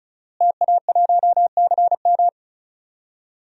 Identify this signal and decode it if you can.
Morse code, heard as a keyed tone, about 34 wpm, 700 Hz. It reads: TA1CM
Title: DXCC-100 Call Signs - 5 Characters - Repeated 2x - 35wpm